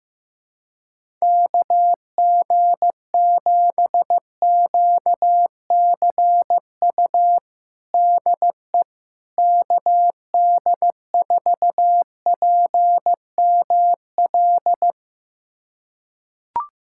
Morse code, heard as a keyed tone, about 15 wpm, 700 Hz. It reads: KG7QCU DE KD4PML
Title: USA Callsign DE Callsign - 6 Characters - 15wpm